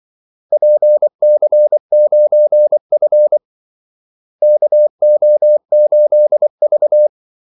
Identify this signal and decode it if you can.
Morse code, heard as a keyed tone, about 24 wpm, 600 Hz. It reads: PC9F KO8V
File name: Callsigns-24wpm